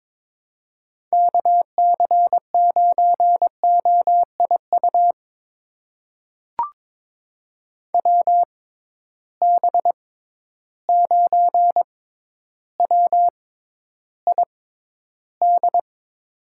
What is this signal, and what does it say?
Morse code, heard as a keyed tone, about 22 wpm, spaced as for 11 wpm, 700 Hz. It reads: KC9OIU WB9WID
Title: USA Call Signs - 6 Characters - 6x Character Spacing - 22wpm